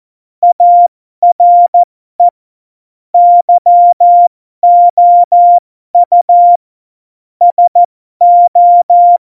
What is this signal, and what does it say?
Morse code, heard as a keyed tone, about 14 wpm, spaced as for 12 wpm, 700 Hz. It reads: ARE YOU SO
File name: Morsekurs / Witze